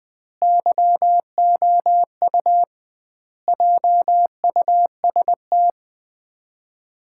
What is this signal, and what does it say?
Morse code, heard as a keyed tone, about 21 wpm, 700 Hz. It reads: YOU JUST
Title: 2x Word-Spacing - Sets of 5 Words - 20wpm